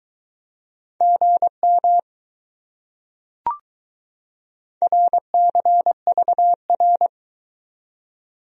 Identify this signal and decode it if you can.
Morse code, heard as a keyed tone, about 22 wpm, 700 Hz. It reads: GM RCVR